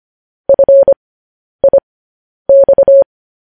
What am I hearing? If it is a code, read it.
Morse code, heard as a keyed tone, about 25 wpm, spaced as for 10 wpm, 550 Hz. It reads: FIX
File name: podcast / News headlines-1x-25x10